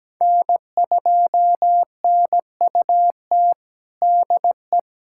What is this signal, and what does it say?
Morse code, heard as a keyed tone, about 17 wpm, 700 Hz. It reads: N2NUT DE
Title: USA Callsign DE Callsign - 5 Characters - 17wpm